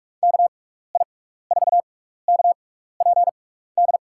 Morse code, heard as a keyed tone, about 44 wpm, spaced as for 15 wpm, 700 Hz. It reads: KI4KPD
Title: USA Call Signs - 6 Characters - 6x Character Spacing - 45wpm